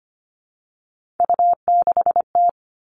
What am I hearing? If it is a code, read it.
Morse code, heard as a keyed tone, about 25 wpm, 700 Hz. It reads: U6T
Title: Three Challenging Letter-Number Combinations 25wpm